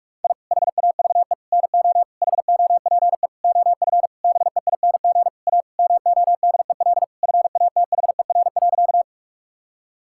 Morse code, heard as a keyed tone, about 41 wpm, 700 Hz. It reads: I HAVE NO HOPE OF BEING A MODEL FATHER.
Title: Sentences from Top 1000 Words - Review Part 5 - 45wpm